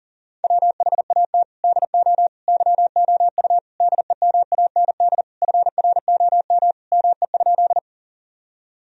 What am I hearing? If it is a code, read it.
Morse code, heard as a keyed tone, about 37 wpm, 700 Hz. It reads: WHAT DO YOU DEMAND FROM ME?
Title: Sentences from Top 1000 Words - Review Part 4 - 40wpm